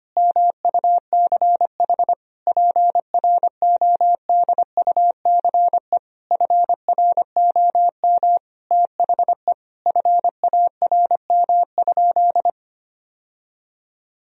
Morse code, heard as a keyed tone, about 25 wpm, 700 Hz. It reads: MUCH PRODUCE FROM THE FARM?
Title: Sentences from Top 1000 Words - Review Part 3 - 25wpm